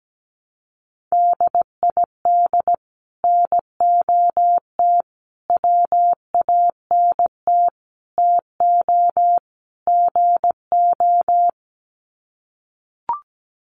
Morse code, heard as a keyed tone, about 17 wpm, 700 Hz. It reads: DID NOT WANT TO GO
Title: Sets of 5 Words 17wpm